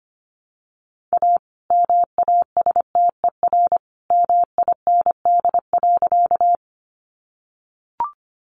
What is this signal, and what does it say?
Morse code, heard as a keyed tone, about 25 wpm, 700 Hz. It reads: A MASTER MIND.
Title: Sentences from Top 1300 Words 25wpm